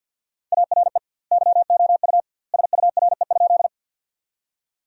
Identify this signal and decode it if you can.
Morse code, heard as a keyed tone, about 48 wpm, 700 Hz. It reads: ARE YOU SURE?
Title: 2x Word-Spacing - Sentences from Top 300 Words - 50wpm